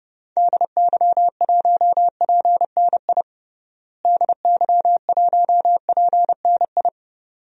Morse code, heard as a keyed tone, about 30 wpm, 700 Hz. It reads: DY1PNI DY1PNI
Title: DXCC-100 Call Signs - 6 Characters - Repeated 2x - 30wpm